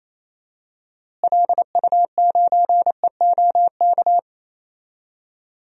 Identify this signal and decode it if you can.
Morse code, heard as a keyed tone, about 28 wpm, 700 Hz. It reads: LU9EOK
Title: DXCC-100 Call Signs - 6 Characters - Repeated 2x - 28wpm